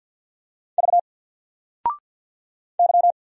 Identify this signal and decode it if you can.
Morse code, heard as a keyed tone, about 48 wpm, 700 Hz. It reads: V Y
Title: Single Letter-Number Alt 50wpm